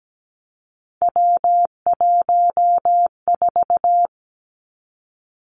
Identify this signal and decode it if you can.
Morse code, heard as a keyed tone, about 17 wpm, 700 Hz. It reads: W14
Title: Three Challenging Letter-Number Combinations 17wpm